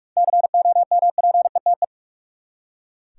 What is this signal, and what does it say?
Morse code, heard as a keyed tone, about 42 wpm, 700 Hz. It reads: COMPETE
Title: Seven Letter Words - 45wpm